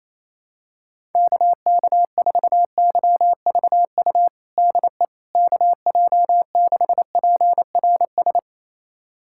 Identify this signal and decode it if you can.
Morse code, heard as a keyed tone, about 28 wpm, 700 Hz. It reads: KK4YVU DE KJ6PRS
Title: USA Callsign DE Callsign 28wpm